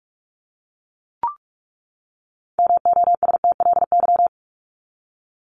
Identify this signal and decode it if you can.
Morse code, heard as a keyed tone, about 44 wpm, 700 Hz. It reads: MOSTLY
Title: Top 1001-1250 Words 45wpm